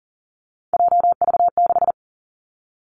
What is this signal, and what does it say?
Morse code, heard as a keyed tone, about 42 wpm, 700 Hz. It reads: JV6
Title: Three Challenging Letter-Number Combinations 40wpm